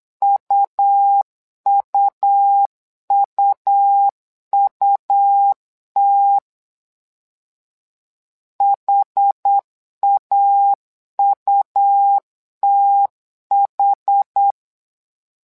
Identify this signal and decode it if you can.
Morse code, heard as a keyed tone, about 8 wpm, 800 Hz. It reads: UUUUT HAUTH
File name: LEZIONI / 2 UHDN / 2 UH 8wpm 2129